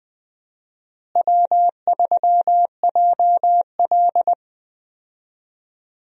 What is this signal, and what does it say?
Morse code, heard as a keyed tone, about 20 wpm, 700 Hz. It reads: W3JL
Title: USA Call Signs - 4 Characters - Repeated 3x - 20wpm